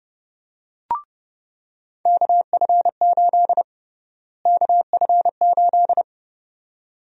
Morse code, heard as a keyed tone, about 30 wpm, 700 Hz. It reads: KF8 KF8
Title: USA Call Sign Prefixes - Repeated 2x - 30wpm